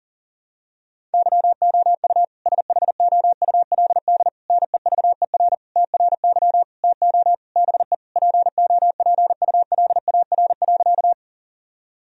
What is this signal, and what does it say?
Morse code, heard as a keyed tone, about 40 wpm, 700 Hz. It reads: YOU SHOULD NEVER TRY TO BE POPULAR.
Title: Sentences from Top 1000 Words - Review Part 3 - 40wpm